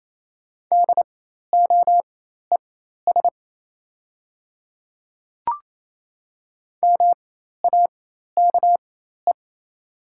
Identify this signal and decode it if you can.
Morse code, heard as a keyed tone, about 28 wpm, spaced as for 13 wpm, 690 Hz. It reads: DOES MAKE
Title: 4x Character Spacing - Top 200 Words - 28wpm